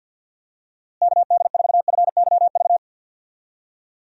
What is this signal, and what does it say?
Morse code, heard as a keyed tone, about 50 wpm, 700 Hz. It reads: KD4FYV
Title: USA Call Signs - 6 Characters - Repeated 3x - 50wpm